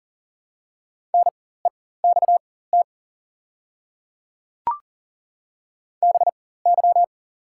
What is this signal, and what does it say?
Morse code, heard as a keyed tone, about 39 wpm, spaced as for 18 wpm, 700 Hz. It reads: NEXT BY
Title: 4x Character Spacing - Top 300 Words - 40wpm